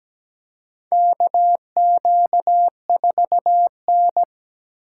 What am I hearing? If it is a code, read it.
Morse code, heard as a keyed tone, about 17 wpm, 700 Hz. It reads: KQ4N
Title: USA Call Signs - 4 Characters - 4x Character Spacing - 17wpm